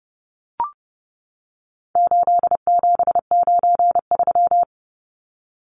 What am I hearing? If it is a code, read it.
Morse code, heard as a keyed tone, about 30 wpm, 700 Hz. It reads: 8793
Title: Numbers - 4-Digit 30wpm